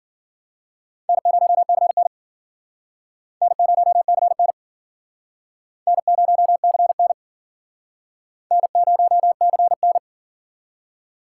Morse code, heard as a keyed tone, about 54 wpm, 700 Hz. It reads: N0CN N0CN N0CN N0CN